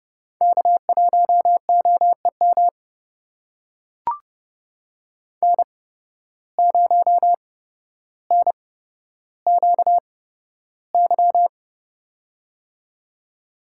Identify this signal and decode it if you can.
Morse code, heard as a keyed tone, about 30 wpm, spaced as for 13 wpm, 700 Hz. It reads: K1OEM N0NQY
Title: USA Call Signs - 5 Characters - 8x Character Spacing - 30wpm